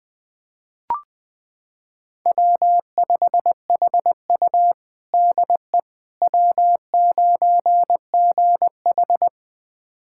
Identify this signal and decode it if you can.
Morse code, heard as a keyed tone, about 20 wpm, 700 Hz. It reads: W5HU DE W9GH
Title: USA Callsign DE Callsign - 4 Characters - 20wpm